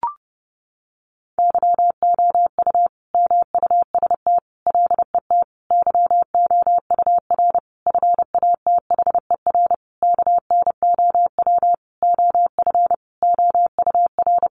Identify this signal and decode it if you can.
Morse code, heard as a keyed tone, about 30 wpm, 700 Hz. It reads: YOU MUST LET YOUR FATHER KNOW OF OUR
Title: Sentences from Top 1500 Words 30wpm